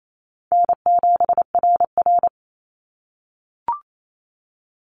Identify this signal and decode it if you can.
Morse code, heard as a keyed tone, about 28 wpm, 700 Hz. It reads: N7RR
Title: NA SOTA Activator Callsigns 28wpm